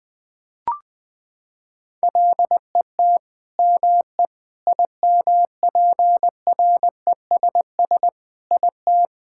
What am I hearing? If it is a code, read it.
Morse code, heard as a keyed tone, about 20 wpm, 700 Hz. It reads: LET ME IMPRESS IT
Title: Sentences from Top 1600 Words 20wpm